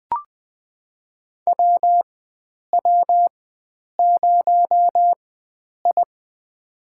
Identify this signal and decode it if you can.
Morse code, heard as a keyed tone, about 20 wpm, spaced as for 9 wpm, 700 Hz. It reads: WW0I
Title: USA Call Signs - 4 Characters - 4x Character Spacing - 20wpm